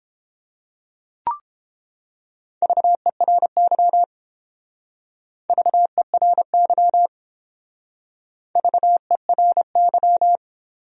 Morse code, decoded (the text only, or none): VERY VERY VERY